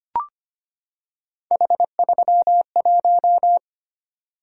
Morse code, heard as a keyed tone, about 25 wpm, 700 Hz. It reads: H31
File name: Three Letter-Number Combinations - 25wpm